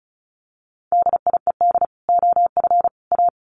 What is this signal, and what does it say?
Morse code, heard as a keyed tone, about 34 wpm, 700 Hz. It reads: DIED OF A
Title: Sets of 5 Words Encore2 35wpm